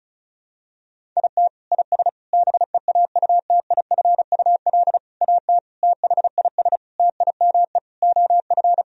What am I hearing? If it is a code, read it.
Morse code, heard as a keyed tone, about 34 wpm, 700 Hz. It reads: IT IS BEAUTIFUL AT THIS TIME OF
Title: Sentences from Top 1000 Words - Review Part 1 - 35wpm